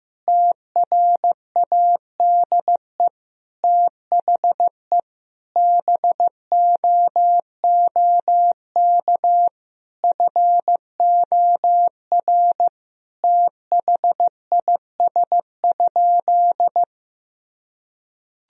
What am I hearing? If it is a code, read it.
Morse code, heard as a keyed tone, about 15 wpm, 700 Hz. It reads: TRADE THE BOOK FOR THIS?
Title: Sentences from Top 1000 Words - Review Part 4 - 15wpm